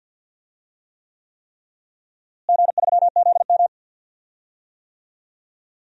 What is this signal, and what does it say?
Morse code, heard as a keyed tone, about 46 wpm, 700 Hz. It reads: G3ZM
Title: DXCC-100 Call Signs - 4 Characters - Repeated 3x - 50wpm